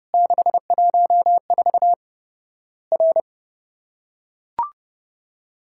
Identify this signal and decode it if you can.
Morse code, heard as a keyed tone, about 30 wpm, 700 Hz. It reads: 614 R
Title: Worked All Europe DX Contest - QTC Exchange - 30wpm